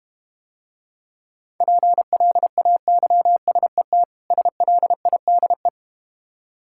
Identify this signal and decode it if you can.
Morse code, heard as a keyed tone, about 31 wpm, 700 Hz. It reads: PLAYSET SLIDE